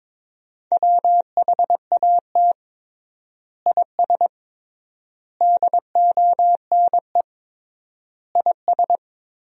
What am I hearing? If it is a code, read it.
Morse code, heard as a keyed tone, about 22 wpm, 700 Hz. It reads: WHAT IS DONE IS
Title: 3x Word-Spacing - Sentences from Top 500 Words - 22wpm